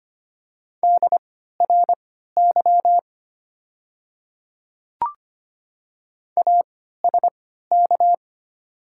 Morse code, heard as a keyed tone, about 25 wpm, spaced as for 14 wpm, 700 Hz. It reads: DRY ASK
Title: 3x Character Spacing - Top 500 Words - 25wpm